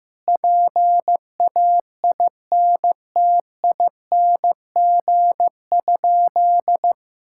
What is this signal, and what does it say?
Morse code, heard as a keyed tone, about 15 wpm, 700 Hz. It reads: PAINTING?
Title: Sentences from Top 1000 Words - Review Part 6 - 15wpm